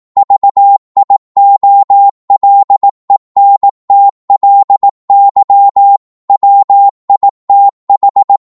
Morse code, heard as a keyed tone, about 18 wpm, 800 Hz. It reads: VIOLENTLY WITH
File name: 18wpm